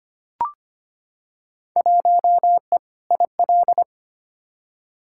Morse code, heard as a keyed tone, about 25 wpm, 700 Hz. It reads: 1E IL
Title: Field Day Exchanges - 25wpm